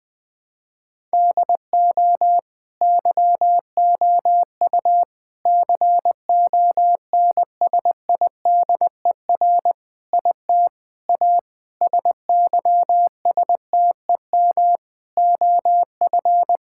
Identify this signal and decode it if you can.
Morse code, heard as a keyed tone, about 21 wpm, 700 Hz. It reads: DO YOU CONSIDER IT A SYSTEM OF
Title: Sentences from Top 1000 Words - Review Part 6 - 20wpm